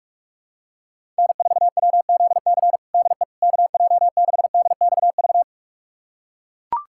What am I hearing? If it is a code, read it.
Morse code, heard as a keyed tone, about 42 wpm, 700 Hz. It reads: N4WZC DE KJ6DXV
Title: USA Callsign DE Callsign 45wpm